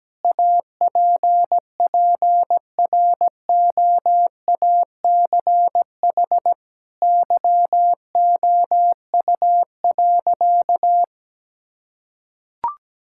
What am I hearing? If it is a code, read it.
Morse code, heard as a keyed tone, about 17 wpm, 700 Hz. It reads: APPROACH YOU.
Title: Sentences from Top 1000 Words - Review Part 6 - 17wpm